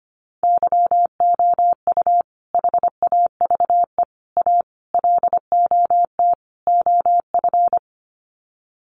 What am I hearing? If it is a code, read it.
Morse code, heard as a keyed tone, about 25 wpm, 700 Hz. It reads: YOU HAVE A LOT OF
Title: Sets of 5 Words Encore 25wpm